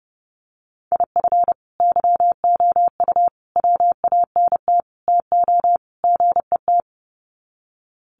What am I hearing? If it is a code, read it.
Morse code, heard as a keyed tone, about 30 wpm, 700 Hz. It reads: IF YOU WANT TO GET
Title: Sets of 5 Words Encore 30wpm